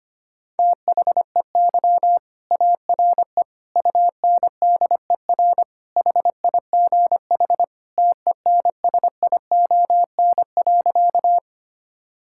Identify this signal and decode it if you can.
Morse code, heard as a keyed tone, about 25 wpm, 700 Hz. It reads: THEY ARE UNDER HIGH TENSION.
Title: Sentences from Top 1600 Words 25wpm